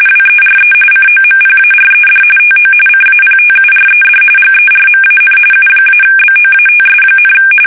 RTTY, about 45.45 baud. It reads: WERE COMPARED TO CAPTURED GERMAN U-BOATS,
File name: fsk850